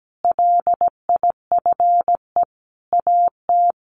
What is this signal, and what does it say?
Morse code, heard as a keyed tone, about 17 wpm, 700 Hz. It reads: LIFE AT
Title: Sentences from Top 1100 Words 17wpm